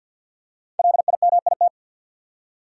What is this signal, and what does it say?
Morse code, heard as a keyed tone, about 50 wpm, 700 Hz. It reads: LIMIT
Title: Five Letter Words - 50wpm